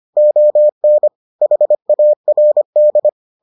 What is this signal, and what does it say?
Morse code, heard as a keyed tone, about 25 wpm, 600 Hz. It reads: ON HARD